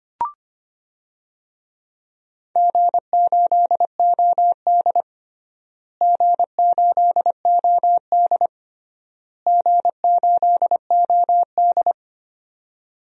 Morse code, heard as a keyed tone, about 25 wpm, 700 Hz. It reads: G8OD G8OD G8OD
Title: DXCC-100 Call Signs - 4 Characters - Repeated 3x - 25wpm